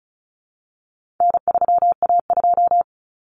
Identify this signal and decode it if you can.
Morse code, heard as a keyed tone, about 34 wpm, 700 Hz. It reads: N3A2
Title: Common Cut Numbers - 4-Digit 35wpm